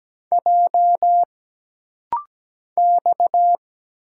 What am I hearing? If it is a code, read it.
Morse code, heard as a keyed tone, about 17 wpm, 700 Hz. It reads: J X
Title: Single Letters - Rapid-Fire 17wpm